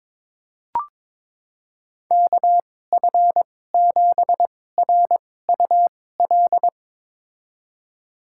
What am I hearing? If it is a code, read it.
Morse code, heard as a keyed tone, about 21 wpm, spaced as for 16 wpm, 700 Hz. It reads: KF7RUL
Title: USA Call Signs - 6 Characters - 2x Character Spacing - 22wpm